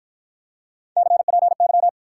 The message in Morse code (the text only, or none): CPY